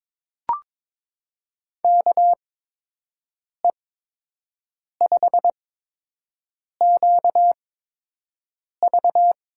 Morse code, heard as a keyed tone, about 22 wpm, spaced as for 6 wpm, 700 Hz. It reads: KE5QV
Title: USA Call Signs - 6 Characters - 8x Character Spacing - 22wpm